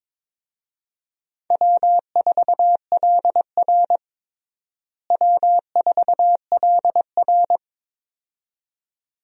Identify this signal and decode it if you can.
Morse code, heard as a keyed tone, about 22 wpm, 700 Hz. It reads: W4LR W4LR
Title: USA Call Signs - 4 Characters - Repeated 2x - 22wpm